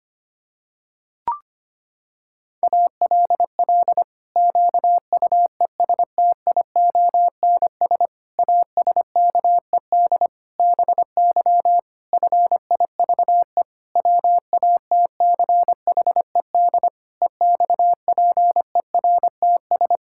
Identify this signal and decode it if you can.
Morse code, heard as a keyed tone, about 25 wpm, 700 Hz. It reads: ALL QUESTIONS ASKED BY FIVE WATCHED EXPERTS
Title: Pangram Challenge v2 - 25wpm